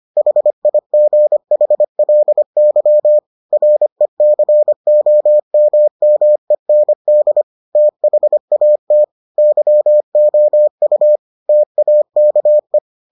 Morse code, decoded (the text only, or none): HIGHLY RECOMMEND THAT YOU TAKE